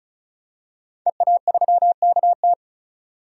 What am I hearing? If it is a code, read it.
Morse code, heard as a keyed tone, about 36 wpm, 700 Hz. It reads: EA3KT
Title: DXCC-100 Call Signs - 5 Characters - 35wpm